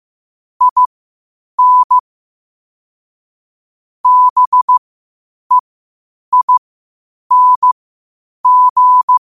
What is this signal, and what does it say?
Morse code, heard as a keyed tone, about 15 wpm, spaced as for 8 wpm, 1000 Hz. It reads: IN BEING
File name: qotdfiles / 8wpm